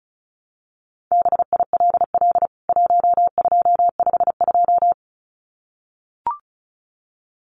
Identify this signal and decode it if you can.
Morse code, heard as a keyed tone, about 34 wpm, 700 Hz. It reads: BILL 1252
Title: CWT Exchange 35wpm